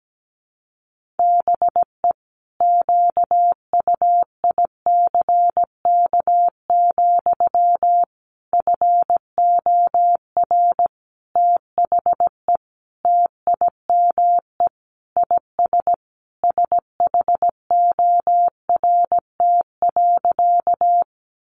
Morse code, heard as a keyed tone, about 17 wpm, 700 Hz. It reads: BE QUICK, FOR THE TIME IS SHORT.
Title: Sentences from Top 500 Words 17wpm